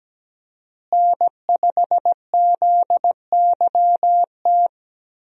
Morse code, heard as a keyed tone, about 17 wpm, 700 Hz. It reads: N5ZYT
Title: USA Call Signs - 5 Characters - Repeated 2x - 17wpm